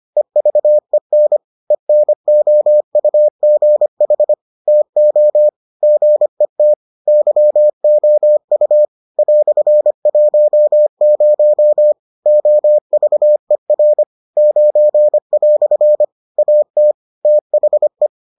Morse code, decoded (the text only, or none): EVEN ENOUGH TO GET YOU "10 OVER 9" AT THE